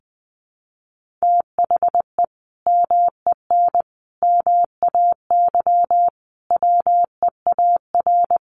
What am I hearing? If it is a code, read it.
Morse code, heard as a keyed tone, about 20 wpm, 700 Hz. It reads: THE MEN MAY WEAR
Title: Sentences from Top 1500 Words 20wpm